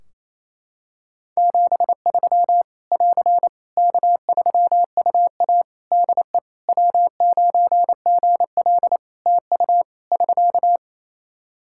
Morse code, heard as a keyed tone, about 27 wpm, 700 Hz. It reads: 73 <AR> K3UA DE W9GL TU <SK>
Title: Practice QSO - Exchange 4 - Ending - 28wpm